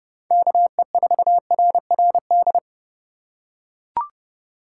Morse code, heard as a keyed tone, about 30 wpm, 700 Hz. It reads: KE4RRD